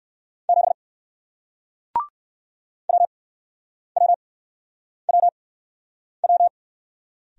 Morse code, heard as a keyed tone, about 57 wpm, spaced as for 9 wpm, 700 Hz. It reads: 7 WWWW